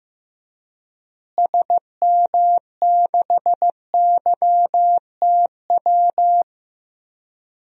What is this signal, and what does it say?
Morse code, heard as a keyed tone, about 15 wpm, 700 Hz. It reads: SM6YTW
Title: DXCC-100 Call Signs - 6 Characters - Repeated 2x - 15wpm